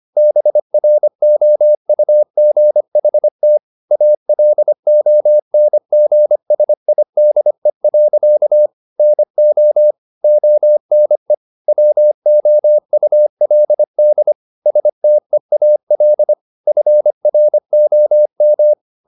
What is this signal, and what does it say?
Morse code, heard as a keyed tone, about 25 wpm, 600 Hz. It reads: BROUGHT ALONGSIDE. NO ONE WOULD STEAL FROM
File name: the old man and the sea part 1 25wpm